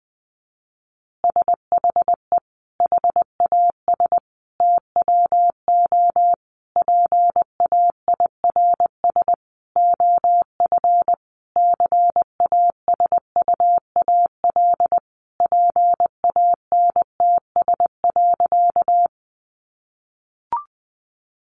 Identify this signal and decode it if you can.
Morse code, heard as a keyed tone, about 20 wpm, 700 Hz. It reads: SHE HAS TWO PAIRS OF CASUAL PANTS.
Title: Additional Easy Phrases 20wpm